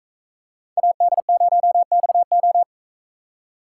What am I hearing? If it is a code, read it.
Morse code, heard as a keyed tone, about 44 wpm, 700 Hz. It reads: AD0XO